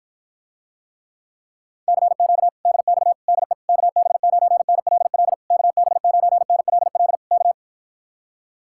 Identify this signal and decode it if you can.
Morse code, heard as a keyed tone, about 53 wpm, 700 Hz. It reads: CQ DX DE KB9NLL KB9NLL K